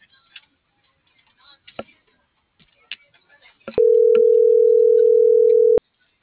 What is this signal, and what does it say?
Ringback tone (North America), 440 + 480 Hz.
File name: CENTRAL AGUILERA / out ram.lmorena.2025 - .12080